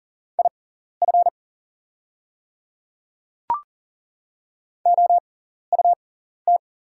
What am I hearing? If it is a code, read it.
Morse code, heard as a keyed tone, about 40 wpm, spaced as for 14 wpm, 700 Hz. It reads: IF OUT
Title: 6x Character Spacing - Top 100 Words - 40wpm